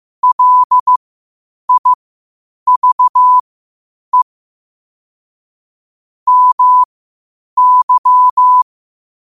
Morse code, heard as a keyed tone, about 15 wpm, spaced as for 8 wpm, 1000 Hz. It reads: LIVE MY
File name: qotdfiles / 8wpm